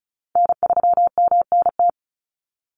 Morse code, heard as a keyed tone, about 34 wpm, 700 Hz. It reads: N3MNT
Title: Call Signs Repeated 3x - 35wpm